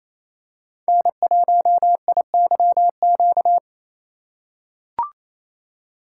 Morse code, heard as a keyed tone, about 27 wpm, 700 Hz. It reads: N1IYQ
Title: USA Call Signs - 5 Characters - 2x Character Spacing - 28wpm